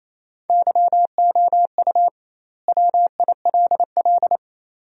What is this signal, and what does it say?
Morse code, heard as a keyed tone, about 28 wpm, 700 Hz. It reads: YOU WILL
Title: 2x Word-Spacing - Sentences from Top 500 Words - 28wpm